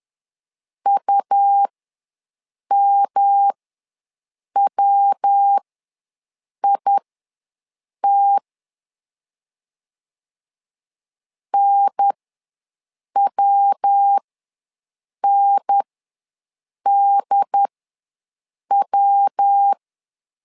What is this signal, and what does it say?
Morse code, heard as a keyed tone, about 10 wpm, spaced as for 6 wpm, 800 Hz. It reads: UMWIT NWNDW
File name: LEZIONI / 3 MWRO / 3 MW ws 10wpm 2213